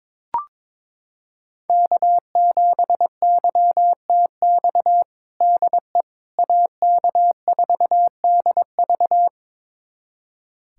Morse code, heard as a keyed tone, about 22 wpm, 700 Hz. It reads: K7YTX DE AK4DV
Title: USA Callsign DE Callsign - 5 Characters - 22wpm